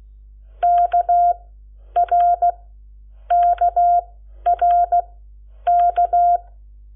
Morse code, heard as a keyed tone, about 16 wpm, spaced as for 9 wpm, 700 Hz. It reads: KRKRK
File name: Lecciones / CLASE Nº7